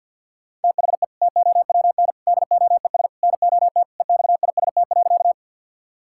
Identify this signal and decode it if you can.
Morse code, heard as a keyed tone, about 46 wpm, 700 Hz. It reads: THE TOWN DOES NOT EXIST.
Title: Sentences from Top 1000 Words - Review Part 5 - 50wpm